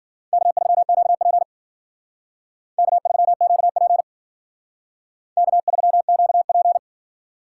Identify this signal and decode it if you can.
Morse code, heard as a keyed tone, about 51 wpm, 700 Hz. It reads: K3QP K3QP K3QP